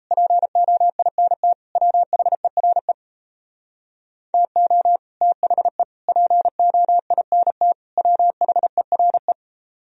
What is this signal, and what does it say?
Morse code, heard as a keyed tone, about 35 wpm, 700 Hz. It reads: POINT WHERE TO THE POINT WHERE